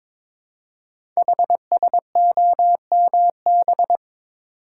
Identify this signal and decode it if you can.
Morse code, heard as a keyed tone, about 22 wpm, 700 Hz. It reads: HSOMB
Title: Code Groups - 5 Letters - 22wpm